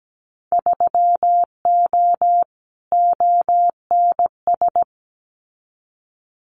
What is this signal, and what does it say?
Morse code, heard as a keyed tone, about 17 wpm, 700 Hz. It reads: 3O ONS
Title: Winter Field Day Exchanges Encore 17wpm